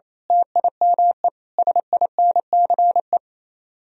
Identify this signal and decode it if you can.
Morse code, heard as a keyed tone, about 30 wpm, 700 Hz. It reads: TIME SINCE
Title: Speed-Racing - Sets of 5 Words 28wpm